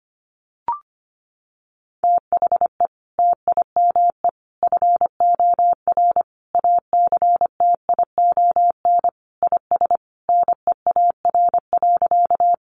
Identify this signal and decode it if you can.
Morse code, heard as a keyed tone, about 25 wpm, 700 Hz. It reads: THE TIME FOR ACTION IS NEAR.
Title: Sentences from Top 700 Words 25wpm